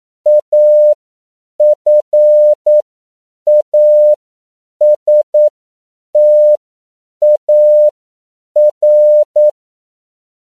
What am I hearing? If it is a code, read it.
Morse code, heard as a keyed tone, about 9 wpm, spaced as for 7 wpm, 600 Hz. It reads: AFASTAR
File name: CW / 10PPM